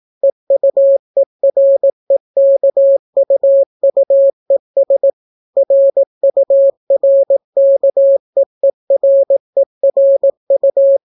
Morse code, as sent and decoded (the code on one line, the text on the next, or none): . ..- . .-. -.- ..- ..- . ... / .-. ..- .-. -.- . .-. .-. ..-
EUEREKUUES RURKEERERU